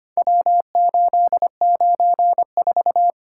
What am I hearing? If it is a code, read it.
Morse code, heard as a keyed tone, about 25 wpm, 700 Hz. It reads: W894
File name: practice / 25wpm / Four Letter-Number Combinations - 25wpm